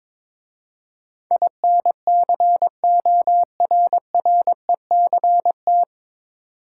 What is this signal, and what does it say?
Morse code, heard as a keyed tone, about 23 wpm, 700 Hz. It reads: INCORRECT
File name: 9 Letter Words - First 3 Letters Given - 22wpm